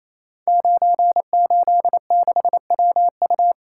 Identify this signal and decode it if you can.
Morse code, heard as a keyed tone, about 27 wpm, 700 Hz. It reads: 986WU
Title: Code Groups - 5 Letter:Numbers - 28wpm